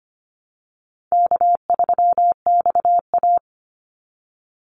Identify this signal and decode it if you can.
Morse code, heard as a keyed tone, about 25 wpm, 700 Hz. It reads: K3XA
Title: Easy Call Signs Repeated 3x - 25wpm